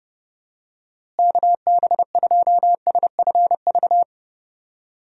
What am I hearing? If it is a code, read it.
Morse code, heard as a keyed tone, about 32 wpm, 700 Hz. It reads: KB2SFV
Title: USA Call Signs - 6 Characters - Repeated 2x - 30wpm